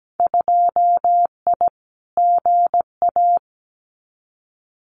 Morse code, heard as a keyed tone, about 17 wpm, 700 Hz. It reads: 2I GA
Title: Winter Field Day Exchanges 17wpm